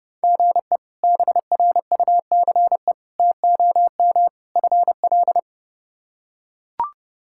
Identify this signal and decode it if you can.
Morse code, heard as a keyed tone, about 32 wpm, 700 Hz. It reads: GE BRUCE TOM FL
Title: K1USN SST QSO Practice - 30wpm